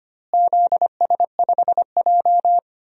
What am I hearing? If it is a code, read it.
Morse code, heard as a keyed tone, about 25 wpm, 700 Hz. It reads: ZS5J
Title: DXCC-100 Call Signs - 4 Characters - Repeated 3x - 25wpm